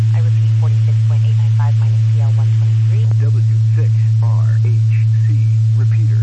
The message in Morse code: none

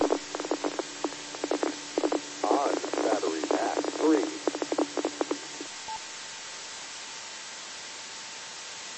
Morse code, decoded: none

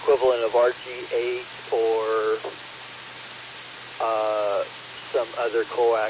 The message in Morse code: none

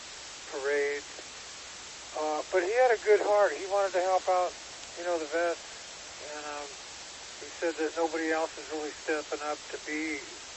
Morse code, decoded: none

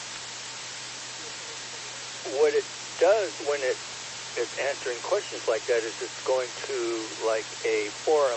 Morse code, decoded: none

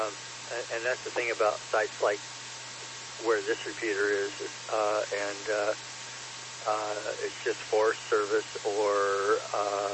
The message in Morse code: none

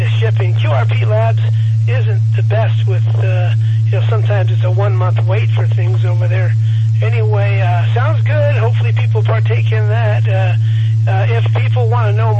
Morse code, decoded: none

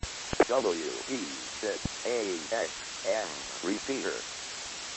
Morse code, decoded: none

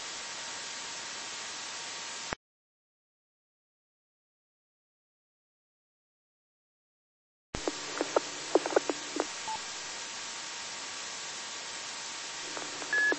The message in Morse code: none

